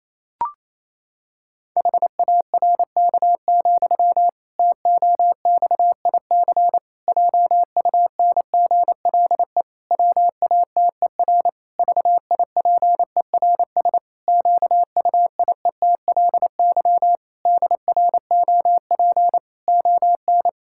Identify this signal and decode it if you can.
Morse code, heard as a keyed tone, about 28 wpm, 700 Hz. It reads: HARK, TOXIC JUNGLE WATER VIPERS QUIETLY DROP ON
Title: Pangram Challenge v2 - 28wpm